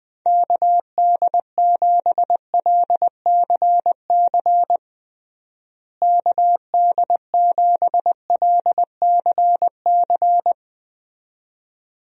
Morse code, heard as a keyed tone, about 20 wpm, 700 Hz. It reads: KD7LCC KD7LCC
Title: USA Call Signs - 6 Characters - Repeated 2x - 20wpm